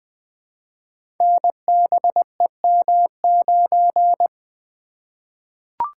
Morse code, heard as a keyed tone, about 20 wpm, 700 Hz. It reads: NBEM9
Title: Code Groups - 5 Letter:Numbers - 20wpm